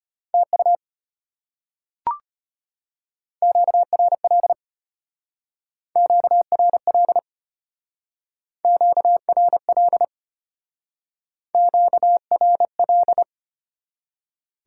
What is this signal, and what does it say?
Morse code, heard as a keyed tone, about 32 wpm, 700 Hz. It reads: TU QRL QRL QRL QRL